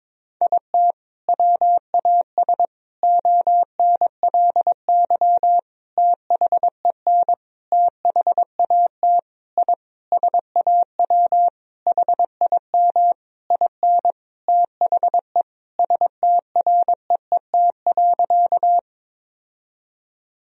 Morse code, heard as a keyed tone, about 22 wpm, 700 Hz. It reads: IT WAS ONLY THEN THAT I SAW HIM IN THE STREET.
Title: Sentences from Top 1000 Words - Review Part 3 - 22wpm